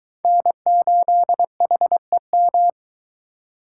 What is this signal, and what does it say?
Morse code, heard as a keyed tone, about 22 wpm, 700 Hz. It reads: N8HEM